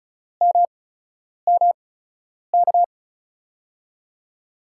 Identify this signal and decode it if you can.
Morse code, heard as a keyed tone, about 36 wpm, spaced as for 10 wpm, 700 Hz. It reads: MMK
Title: USA Call Signs - 6 Characters - 8x Character Spacing - 35wpm